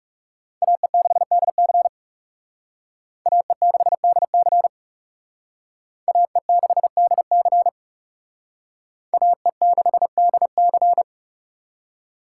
Morse code, heard as a keyed tone, about 38 wpm, 700 Hz. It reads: AE6DC AE6DC AE6DC AE6DC